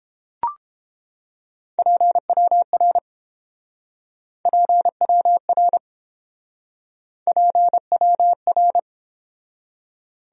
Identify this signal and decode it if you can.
Morse code, heard as a keyed tone, about 28 wpm, 700 Hz. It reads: PWR PWR PWR